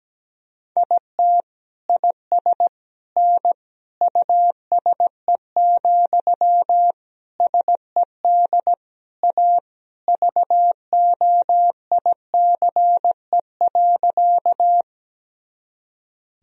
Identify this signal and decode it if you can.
Morse code, heard as a keyed tone, about 17 wpm, 700 Hz. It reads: IT IS N USE, SED A VOICE.
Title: Sentences from Top 500 Words and Abbreviations 17wpm